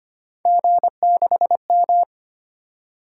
Morse code, heard as a keyed tone, about 25 wpm, 700 Hz. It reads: G6M
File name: Three Letter-Number Combinations - 25wpm